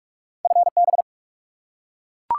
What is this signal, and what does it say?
Morse code, heard as a keyed tone, about 44 wpm, 700 Hz. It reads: FB
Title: Top 180 QSO Elements 45wpm